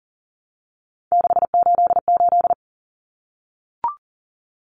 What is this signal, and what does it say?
Morse code, heard as a keyed tone, about 40 wpm, 700 Hz. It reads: 688
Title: Numbers - 3-Digit 40wpm